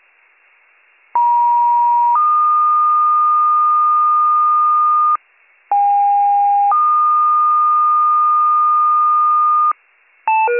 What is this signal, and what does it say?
Two-tone page, heard as a one-second tone, then a three-second tone, about 950 then 1220 Hz, about 810 then 1220 Hz. It is heard from 1.1 to 9.7 s.